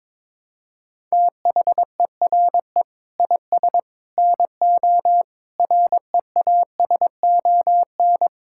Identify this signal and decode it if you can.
Morse code, heard as a keyed tone, about 21 wpm, 700 Hz. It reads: THERE IS NO REASON
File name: practice / 22wpm / Sets of 8 Words - 22wpm